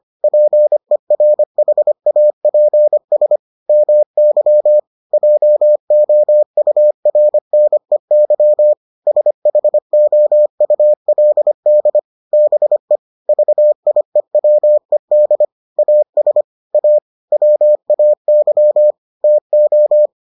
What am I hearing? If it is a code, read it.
Morse code, heard as a keyed tone, about 25 wpm, 600 Hz. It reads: PERHAPS MY JOURNEY SHOULD BE VIEWED AS A WAY TO